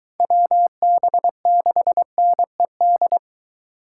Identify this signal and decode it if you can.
Morse code, heard as a keyed tone, about 20 wpm, 700 Hz. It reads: WB6NED